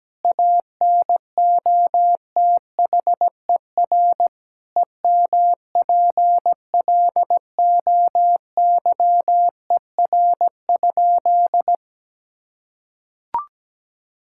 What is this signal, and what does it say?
Morse code, heard as a keyed tone, about 17 wpm, 700 Hz. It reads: ANOTHER EMPLOYER?
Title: Sentences from Top 1600 Words 17wpm